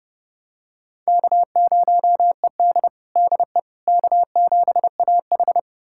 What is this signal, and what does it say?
Morse code, heard as a keyed tone, about 30 wpm, 700 Hz. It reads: K0ED DE K7AH
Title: USA Callsign DE Callsign - 4 Characters - 30wpm